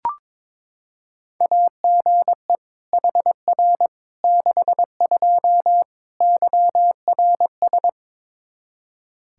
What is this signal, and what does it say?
Morse code, heard as a keyed tone, about 21 wpm, 700 Hz. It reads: AGE HR 62 YRS
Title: Practice QSO - Exchange 3 - More Info - 22wpm